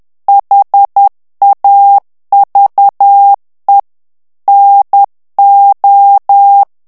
Morse code, heard as a keyed tone, about 10 wpm, 800 Hz. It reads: HAVE NO
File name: morse mp3 / ANGLAIS texte clair 2 vitesse 10 mots minute